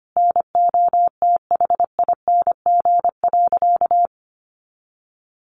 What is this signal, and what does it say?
Morse code, heard as a keyed tone, about 25 wpm, 700 Hz. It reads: NOTHING.
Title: Sentences from Top 500 Words 25wpm